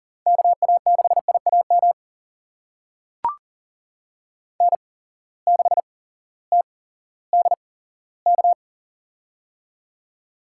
Morse code, heard as a keyed tone, about 40 wpm, spaced as for 18 wpm, 690 Hz. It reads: KA6IAM N6TDK
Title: USA Call Signs - 8x Character Spacing - 40wpm